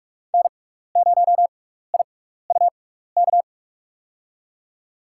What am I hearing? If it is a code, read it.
Morse code, heard as a keyed tone, about 46 wpm, spaced as for 16 wpm, 700 Hz. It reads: N0IUK
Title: USA Call Signs - 5 Characters - 6x Character Spacing - 45wpm